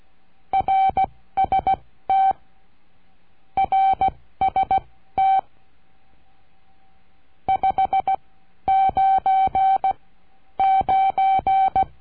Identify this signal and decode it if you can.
Morse code, heard as a keyed tone, about 16 wpm, spaced as for 12 wpm, 770 Hz. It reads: RST RST 599
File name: cursos / CURSO DE TELEGRAFIA / Telegrafia - Trilha 22